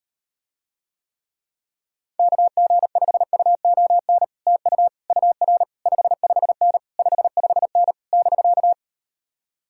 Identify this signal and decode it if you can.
Morse code, heard as a keyed tone, about 38 wpm, 700 Hz. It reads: KG5UON TU UR 55N 55N <BK>